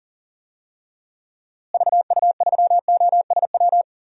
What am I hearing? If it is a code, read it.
Morse code, heard as a keyed tone, about 40 wpm, 700 Hz. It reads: VU3OSW
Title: DXCC-100 Call Signs - 6 Characters - Repeated 2x - 40wpm